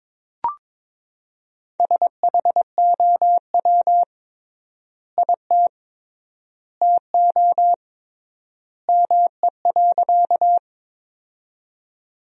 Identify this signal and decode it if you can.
Morse code, heard as a keyed tone, about 22 wpm, 700 Hz. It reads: SHOW IT TO ME.
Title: 3x Word-Spacing - Sentences from Top 200 Words - 22wpm